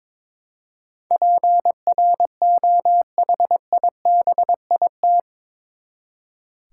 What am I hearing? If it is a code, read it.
Morse code, heard as a keyed tone, about 22 wpm, 700 Hz. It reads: PROHIBIT
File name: practice / 22wpm / Eight Letter Words - 22wpm